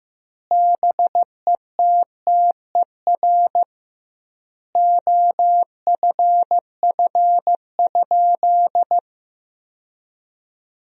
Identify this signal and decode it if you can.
Morse code, heard as a keyed tone, about 15 wpm, 700 Hz. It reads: BETTER OFF?
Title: 2x Word-Spacing - Sentences from Top 500 Words - 15wpm